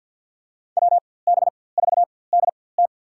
Morse code, heard as a keyed tone, about 47 wpm, spaced as for 23 wpm, 700 Hz. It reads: WB4DT
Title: USA Call Signs - 5 Characters - 4x Character Spacing - 50wpm